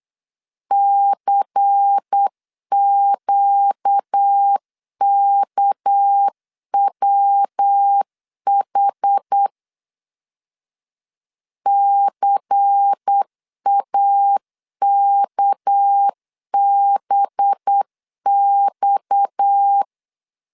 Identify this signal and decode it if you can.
Morse code, heard as a keyed tone, about 8 wpm, 800 Hz. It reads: CQKWH CAKBX
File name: CW / DET / 1 6 1500C S C 8wpm 5219